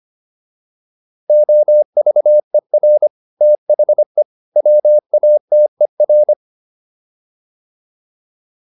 Morse code, heard as a keyed tone, about 25 wpm, 600 Hz. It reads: OVER THE WATER